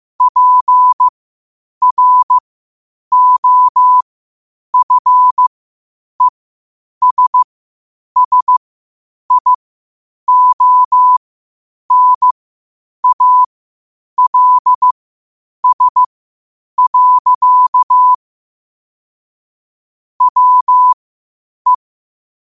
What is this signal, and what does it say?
Morse code, heard as a keyed tone, about 15 wpm, spaced as for 8 wpm, 1000 Hz. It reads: PROFESSIONALS. WE
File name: qotdfiles / 8wpm